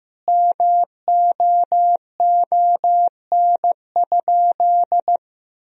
Morse code, heard as a keyed tone, about 15 wpm, 700 Hz. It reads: MOON?
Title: Sentences from Top 1000 Words - Review Part 3 - 15wpm